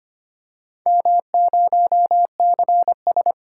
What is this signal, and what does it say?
Morse code, heard as a keyed tone, about 25 wpm, 700 Hz. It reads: M0CS
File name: practice / 25wpm / Four Letter-Number Combinations - 25wpm